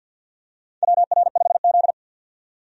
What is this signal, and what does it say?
Morse code, heard as a keyed tone, about 47 wpm, 700 Hz. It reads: WR5Z
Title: USA Call Signs - 4 Characters - 50wpm